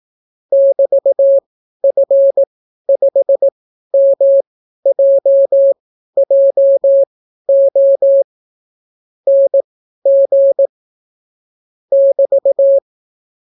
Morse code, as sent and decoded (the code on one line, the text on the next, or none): -...- ..-. ..... -- .--- .--- --- / -. --. / -...-
<BT>F5MJJO NG <BT>